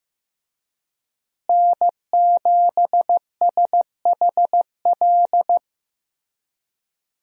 Morse code, heard as a keyed tone, about 15 wpm, 700 Hz. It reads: N7SHL
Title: USA Call Signs - 5 Characters - Repeated 2x - 15wpm